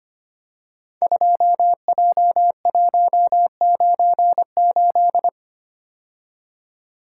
Morse code, heard as a keyed tone, about 25 wpm, 700 Hz. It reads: 2J198
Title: Code Groups - 5 Letter:Numbers - 25wpm